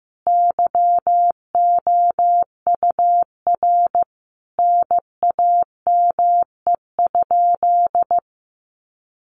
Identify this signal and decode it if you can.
Morse code, heard as a keyed tone, about 15 wpm, 700 Hz. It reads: YOUR NAME?
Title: Sentences from Top 1200 Words 15wpm